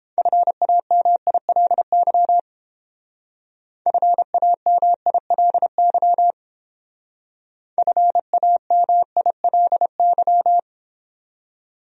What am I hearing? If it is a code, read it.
Morse code, heard as a keyed tone, about 28 wpm, 700 Hz. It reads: FAMILY FAMILY FAMILY